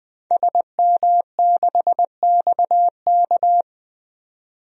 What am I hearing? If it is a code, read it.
Morse code, heard as a keyed tone, about 20 wpm, 700 Hz. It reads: SM6XK
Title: DXCC-100 Call Signs - 5 Characters - 20wpm